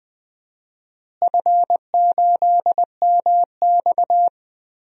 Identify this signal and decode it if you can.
Morse code, heard as a keyed tone, about 21 wpm, 700 Hz. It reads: F8MX
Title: DXCC-100 Call Signs - 4 Characters - Repeated 3x - 20wpm